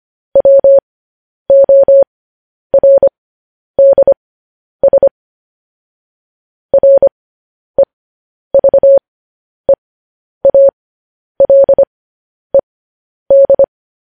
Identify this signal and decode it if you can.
Morse code, heard as a keyed tone, about 25 wpm, spaced as for 10 wpm, 550 Hz. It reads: WORDS REVEALED